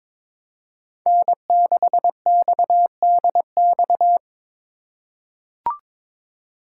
Morse code, heard as a keyed tone, about 22 wpm, 700 Hz. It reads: N6XDX
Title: USA Call Signs - 5 Characters - 2x Character Spacing - 22wpm